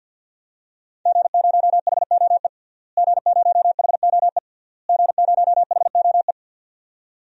Morse code, heard as a keyed tone, about 46 wpm, 700 Hz. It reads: G0HOE G0HOE G0HOE
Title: DXCC-100 Call Signs - 5 Characters - Repeated 3x - 50wpm